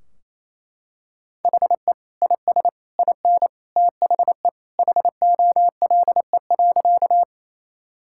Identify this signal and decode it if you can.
Morse code, heard as a keyed tone, about 28 wpm, 700 Hz. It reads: HE IS IN THE HOLE.
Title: Sentences from Top 1600 Words 28wpm